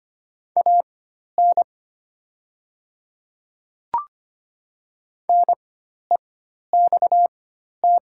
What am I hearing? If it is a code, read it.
Morse code, heard as a keyed tone, about 25 wpm, spaced as for 12 wpm, 700 Hz. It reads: AN NEXT
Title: 4x Character Spacing - Top 300 Words - 25wpm